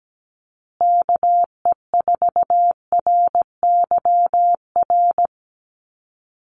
Morse code, heard as a keyed tone, about 17 wpm, 700 Hz. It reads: KE4RYR
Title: USA Call Signs Repeated 3x 17wpm